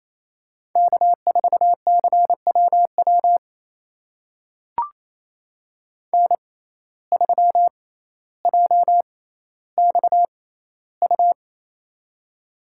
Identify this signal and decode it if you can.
Morse code, heard as a keyed tone, about 28 wpm, spaced as for 14 wpm, 700 Hz. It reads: K4CWW N3JXU
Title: USA Call Signs - 5 Characters - 6x Character Spacing - 28wpm